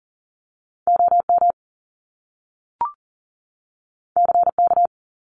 Morse code, heard as a keyed tone, about 39 wpm, 700 Hz. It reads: OM CK
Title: Top Two Letter Combinations in Words 40wpm